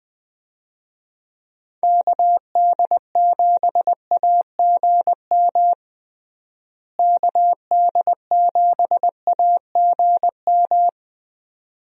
Morse code, heard as a keyed tone, about 21 wpm, 700 Hz. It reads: KD7AGM KD7AGM
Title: USA Call Signs - 6 Characters - Repeated 2x - 20wpm